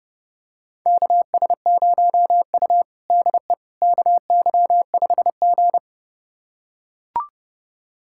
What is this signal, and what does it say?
Morse code, heard as a keyed tone, about 29 wpm, 700 Hz. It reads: KS0U DE KY5G
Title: USA Callsign DE Callsign - 4 Characters - 30wpm